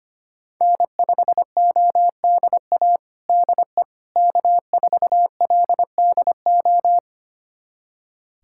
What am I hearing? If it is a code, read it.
Morse code, heard as a keyed tone, about 25 wpm, 700 Hz. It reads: N5ODA DE K4LDO
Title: USA Callsign DE Callsign - 5 Characters - 25wpm